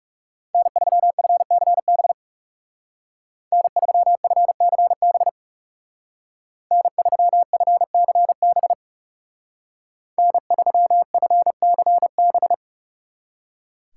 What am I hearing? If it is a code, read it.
Morse code, heard as a keyed tone, about 37 wpm, 700 Hz. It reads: N3FCB N3FCB N3FCB N3FCB
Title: Speed Racing - USA 1x3 Call Signs 30wpm